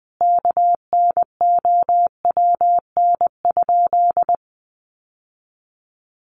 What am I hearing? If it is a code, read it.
Morse code, heard as a keyed tone, about 19 wpm, 700 Hz. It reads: KNOWN?
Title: Sentences from Top 1200 Words 20wpm